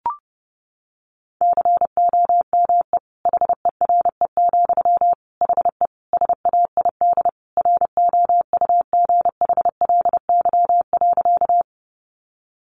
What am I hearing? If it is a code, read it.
Morse code, heard as a keyed tone, about 30 wpm, 700 Hz. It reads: COME HERE, HE SAID ROUGHLY.
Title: Sentences from Top 1500 Words 30wpm